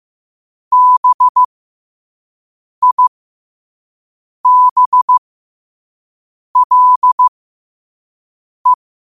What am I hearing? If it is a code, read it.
Morse code, heard as a keyed tone, about 15 wpm, spaced as for 5 wpm, 1000 Hz. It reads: BIBLE
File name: 5wpm